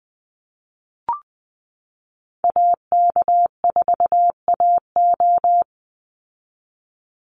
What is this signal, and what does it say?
Morse code, heard as a keyed tone, about 19 wpm, 700 Hz. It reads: AK4AO
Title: Call Signs Repeated 3x - 20wpm